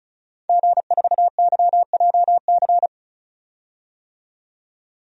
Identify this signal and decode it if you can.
Morse code, heard as a keyed tone, about 34 wpm, 700 Hz. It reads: G4YJC
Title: DXCC-100 Call Signs - 5 Characters - 35wpm